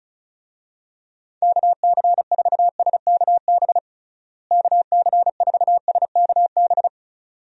Morse code, heard as a keyed tone, about 34 wpm, 700 Hz. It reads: KC4SKB KC4SKB
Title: USA Call Signs - 6 Characters - Repeated 2x - 35wpm